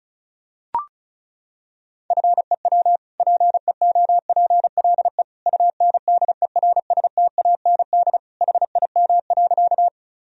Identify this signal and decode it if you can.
Morse code, heard as a keyed tone, about 34 wpm, 700 Hz. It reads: FEW PEOPLE UNDERSTAND HIM.
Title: Sentences from Top 1000 Words - Review Part 1 - 35wpm